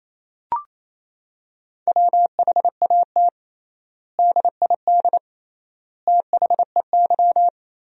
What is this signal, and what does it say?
Morse code, heard as a keyed tone, about 28 wpm, 700 Hz. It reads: WHAT DID THEY
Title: 3x Word-Spacing - Sentences from Top 100 Words - 28wpm